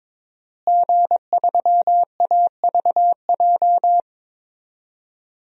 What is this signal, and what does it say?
Morse code, heard as a keyed tone, about 22 wpm, 700 Hz. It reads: G3AVJ
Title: DXCC-100 Call Signs - 5 Characters - Repeated 2x - 22wpm